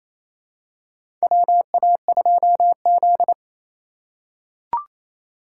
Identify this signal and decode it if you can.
Morse code, heard as a keyed tone, about 28 wpm, 700 Hz. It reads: WA2Z
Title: USA Call Signs - 4x Character Spacing - 28wpm